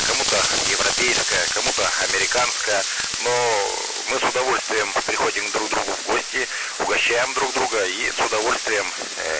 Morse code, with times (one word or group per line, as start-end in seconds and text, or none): none